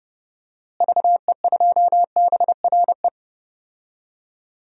Morse code, heard as a keyed tone, about 31 wpm, 700 Hz. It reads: VE2BRE
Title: DXCC-100 Call Signs - 6 Characters - Repeated 3x - 30wpm